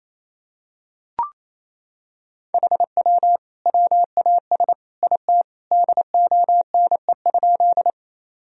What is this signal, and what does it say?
Morse code, heard as a keyed tone, about 28 wpm, 700 Hz. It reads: HW WAS IT DONE?
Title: Sentences from Top 500 Words and Abbreviations 28wpm